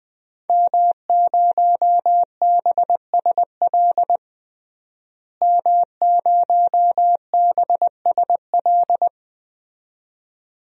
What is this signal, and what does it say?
Morse code, heard as a keyed tone, about 20 wpm, 700 Hz. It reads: M0BSL M0BSL
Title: DXCC-100 Call Signs - 5 Characters - Repeated 2x - 20wpm